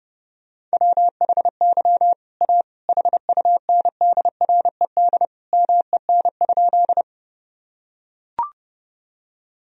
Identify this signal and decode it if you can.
Morse code, heard as a keyed tone, about 32 wpm, 700 Hz. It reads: WHY A HUNDRED MEN?
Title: Sentences from Top 1000 Words - Review Part 2 - 30wpm